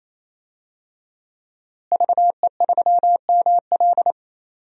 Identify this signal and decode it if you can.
Morse code, heard as a keyed tone, about 28 wpm, 700 Hz. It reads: VE3ML
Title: DXCC-100 Call Signs - 5 Characters - Repeated 3x - 28wpm